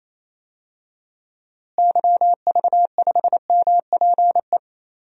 Morse code, heard as a keyed tone, about 28 wpm, 700 Hz. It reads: YV5MPE
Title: DXCC-100 Call Signs - 6 Characters - Repeated 3x - 28wpm